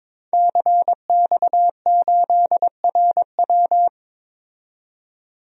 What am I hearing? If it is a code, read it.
Morse code, heard as a keyed tone, about 22 wpm, 700 Hz. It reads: CX8RW
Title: DXCC-100 Call Signs - 5 Characters - 22wpm